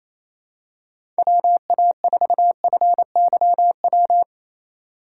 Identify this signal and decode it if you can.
Morse code, heard as a keyed tone, about 28 wpm, 700 Hz. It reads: WA4FYW
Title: USA Call Signs - 3x Character Spacing - 28wpm